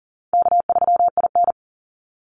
Morse code, heard as a keyed tone, about 40 wpm, 700 Hz. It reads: K3IN
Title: Easy Call Signs Repeated 2x - 40wpm